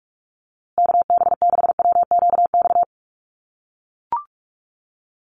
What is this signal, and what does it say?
Morse code, heard as a keyed tone, about 44 wpm, 690 Hz. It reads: KB6WQX
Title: USA Call Signs Repeated 3x 45wpm